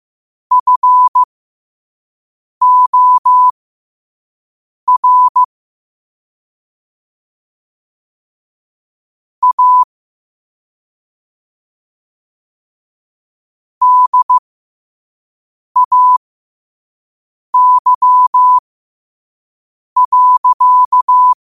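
Morse code, heard as a keyed tone, about 15 wpm, spaced as for 5 wpm, 1000 Hz. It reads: FOR A DAY.
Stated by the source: Morse code quote of the day sent at 5 wpm.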